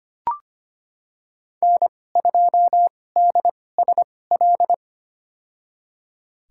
Morse code, heard as a keyed tone, about 25 wpm, spaced as for 18 wpm, 700 Hz. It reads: N2DSL
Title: USA Call Signs - 5 Characters - 2x Character Spacing - 25wpm